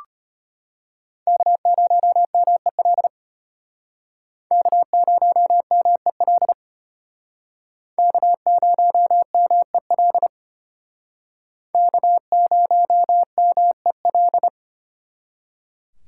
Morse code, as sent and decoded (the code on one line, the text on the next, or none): -.- ----- -- . .-.. / -.- ----- -- . .-.. / -.- ----- -- . .-.. / -.- ----- -- . .-..
K0MEL K0MEL K0MEL K0MEL